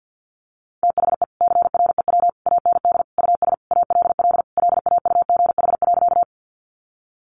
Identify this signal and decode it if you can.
Morse code, heard as a keyed tone, about 50 wpm, 700 Hz. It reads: THE CREW AND US ALL LAUGH.
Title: Sentences from Top 1500 Words 50wpm